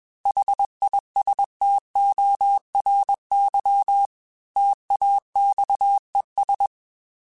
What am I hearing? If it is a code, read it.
Morse code, heard as a keyed tone, about 21 wpm, 800 Hz. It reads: HISTORY TAXES
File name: T then H Runs / T then H Runs.mp3s / T then H Runs @20